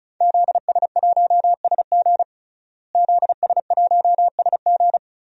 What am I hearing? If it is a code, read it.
Morse code, heard as a keyed tone, about 36 wpm, 700 Hz. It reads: ZS1SG ZS1SG
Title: DXCC-100 Call Signs - 5 Characters - Repeated 2x - 35wpm